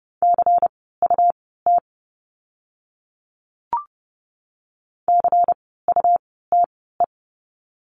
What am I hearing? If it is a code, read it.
Morse code, heard as a keyed tone, about 30 wpm, spaced as for 17 wpm, 700 Hz. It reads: CUT CUTE
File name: Lesson 17f - Letter u - Words 30wpm